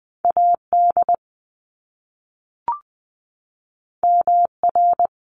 Morse code, heard as a keyed tone, about 20 wpm, 700 Hz. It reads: AD MR
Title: Top Two Letter Combinations in Words 20wpm